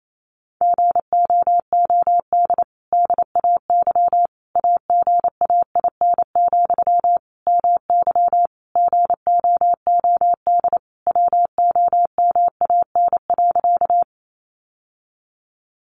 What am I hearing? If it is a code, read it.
Morse code, heard as a keyed tone, about 28 wpm, 700 Hz. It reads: GOOD DAY AGAIN, MY GOOD WOMAN.
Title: Sentences from Top 1100 Words 28wpm